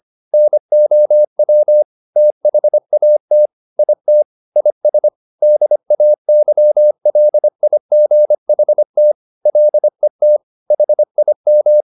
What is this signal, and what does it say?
Morse code, heard as a keyed tone, about 25 wpm, 600 Hz. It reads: NOW THAT IT IS DAYLIGHT LET HIM
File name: the old man and the sea part 1 25wpm